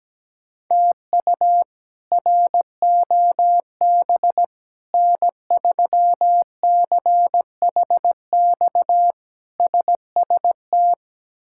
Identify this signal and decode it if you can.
Morse code, heard as a keyed tone, about 17 wpm, 700 Hz. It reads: TU ROB N3CHX SST
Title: K1USN SST QSO Practice - 17wpm